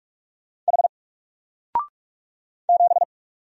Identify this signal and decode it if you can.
Morse code, heard as a keyed tone, about 44 wpm, 700 Hz. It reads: H 7
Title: Single Letter-Number 45wpm